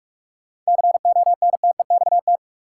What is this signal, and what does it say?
Morse code, heard as a keyed tone, about 46 wpm, 700 Hz. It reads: CONTEXT
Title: Sentences from Top 1000 Words - Review Part 7 - New Word Drill - 45wpm